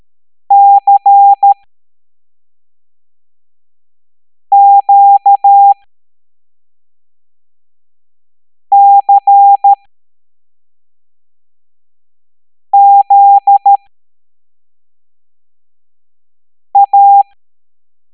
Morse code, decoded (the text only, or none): CQCZA